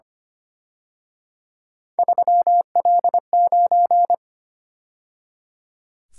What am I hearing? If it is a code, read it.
Morse code, heard as a keyed tone, about 29 wpm, 700 Hz. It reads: 3L9
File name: Speed-Racing - Three Letter-Number Combinations - 25wpm